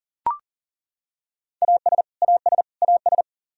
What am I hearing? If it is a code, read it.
Morse code, heard as a keyed tone, about 40 wpm, 700 Hz. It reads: AS AS AS
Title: Top 100 Words Repeated 40wpm